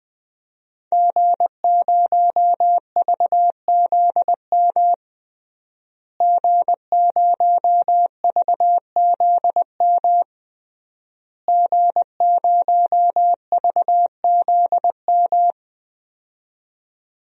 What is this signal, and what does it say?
Morse code, heard as a keyed tone, about 21 wpm, 700 Hz. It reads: G0VZM G0VZM G0VZM
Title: DXCC-100 Call Signs - 5 Characters - Repeated 3x - 20wpm